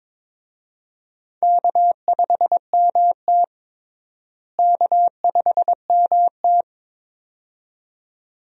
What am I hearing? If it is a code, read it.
Morse code, heard as a keyed tone, about 22 wpm, 700 Hz. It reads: K5MT K5MT
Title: USA Call Signs - 4 Characters - Repeated 2x - 22wpm